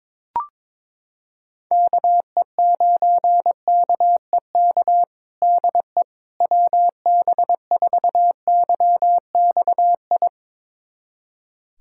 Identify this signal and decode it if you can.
Morse code, heard as a keyed tone, about 22 wpm, 700 Hz. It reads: KE9KEK DE WB4YXI
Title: USA Callsign DE Callsign - 6 Characters - 22wpm